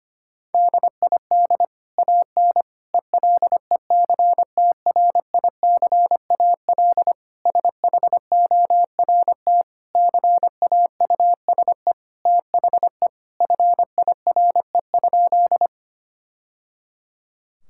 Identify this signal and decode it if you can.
Morse code, heard as a keyed tone, about 25 wpm, 700 Hz. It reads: DID AN ELECTRICAL SHORT CAUSE THE FIRE?
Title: Sentences from Top 1000 Words - Review Part 6 - 25wpm